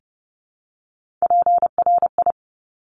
Morse code, heard as a keyed tone, about 30 wpm, 690 Hz. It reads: PRI
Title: Top Three Letter Combinations in Words 30wpm